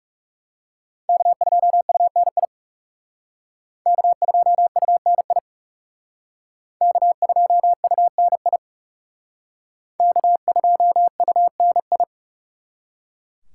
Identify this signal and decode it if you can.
Morse code, heard as a keyed tone, about 37 wpm, 700 Hz. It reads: K2UNI K2UNI K2UNI K2UNI
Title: Speed Racing - USA 1x3 Call Signs 30wpm